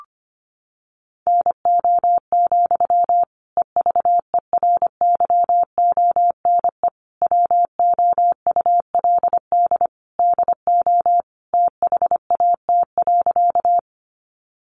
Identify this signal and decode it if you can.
Morse code, heard as a keyed tone, about 25 wpm, 700 Hz. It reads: NO, EVERYONE WOULD DO THAT.
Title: Sentences from Top 800 Words 25wpm